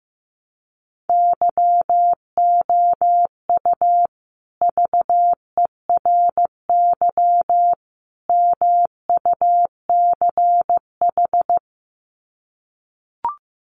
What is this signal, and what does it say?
Morse code, heard as a keyed tone, about 14 wpm, 700 Hz. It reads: YOU VERY MUCH
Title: Sets of 3 Words 15wpm